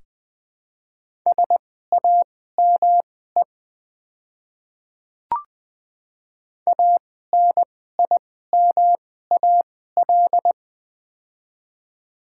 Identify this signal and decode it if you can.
Morse code, heard as a keyed tone, about 20 wpm, spaced as for 14 wpm, 700 Hz. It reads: SAME ANIMAL
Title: 2x Character Spacing - Top 300 Words - 20wpm